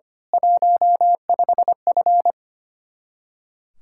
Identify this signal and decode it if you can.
Morse code, heard as a keyed tone, about 24 wpm, 700 Hz. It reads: 15F
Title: Speed-Racing Common QSO Phrases - 25wpm